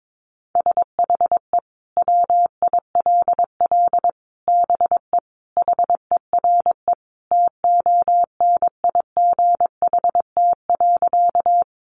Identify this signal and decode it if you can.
Morse code, heard as a keyed tone, about 22 wpm, 700 Hz. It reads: SHE WILL BE HERE TONIGHT.
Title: Sentences from Top 600 Words 22wpm